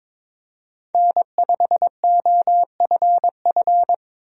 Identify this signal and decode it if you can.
Morse code, heard as a keyed tone, about 22 wpm, 700 Hz. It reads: N5OFF
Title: USA Call Signs - 5 Characters - 22wpm